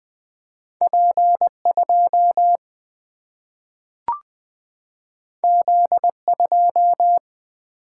Morse code, heard as a keyed tone, about 20 wpm, 700 Hz. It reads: P2 Z2
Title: Single-Letter Single-Number 20wpm